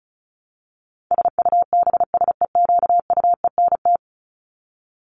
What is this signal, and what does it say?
Morse code, heard as a keyed tone, about 34 wpm, 700 Hz. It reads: SUBSEQUENT
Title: LY Suffix - Top 100 Words with Base Words 35wpm